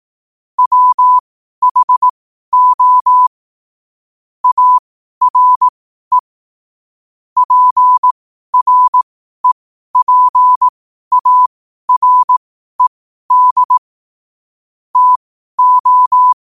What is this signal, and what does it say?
Morse code, heard as a keyed tone, about 18 wpm, spaced as for 12 wpm, 1000 Hz. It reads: WHO ARE PREPARED TO
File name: qotdfiles / 12wpm